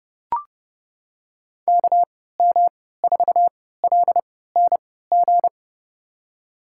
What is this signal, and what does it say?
Morse code, heard as a keyed tone, about 31 wpm, spaced as for 17 wpm, 700 Hz. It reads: KM4LNG
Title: USA Call Signs - 6 Characters - 3x Character Spacing - 30wpm